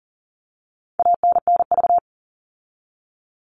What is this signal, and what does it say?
Morse code, heard as a keyed tone, about 40 wpm, 700 Hz. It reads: ANNV
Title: All Cut Numbers - 4-Digit 40wpm